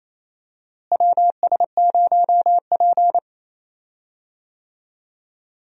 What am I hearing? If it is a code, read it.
Morse code, heard as a keyed tone, about 28 wpm, 700 Hz. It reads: WS0P
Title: USA Call Signs - 4 Characters - Repeated 3x - 28wpm